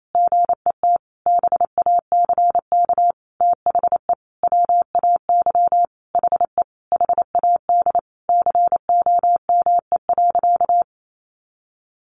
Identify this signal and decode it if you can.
Morse code, heard as a keyed tone, about 28 wpm, 700 Hz. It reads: GET BACK THE WAY HE HAD COME.
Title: Sentences from Top 1100 Words 28wpm